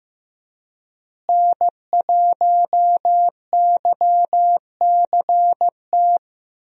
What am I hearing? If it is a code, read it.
Morse code, heard as a keyed tone, about 15 wpm, 700 Hz. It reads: N1YCT
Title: USA Call Signs - 5 Characters - Repeated 3x - 15wpm